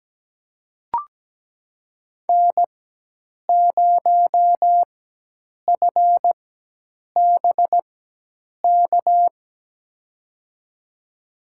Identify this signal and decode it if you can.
Morse code, heard as a keyed tone, about 17 wpm, spaced as for 8 wpm, 700 Hz. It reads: N0FBK
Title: USA Call Signs - 4x Character Spacing - 17wpm